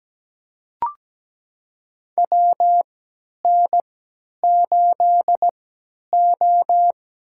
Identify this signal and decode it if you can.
Morse code, heard as a keyed tone, about 17 wpm, spaced as for 10 wpm, 700 Hz. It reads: WN8O
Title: USA Call Signs - 3x Character Spacing - 17wpm